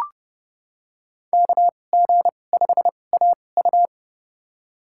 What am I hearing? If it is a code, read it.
Morse code, heard as a keyed tone, about 30 wpm, spaced as for 22 wpm, 700 Hz. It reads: KG5AU
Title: USA Call Signs - 5 Characters - 2x Character Spacing - 30wpm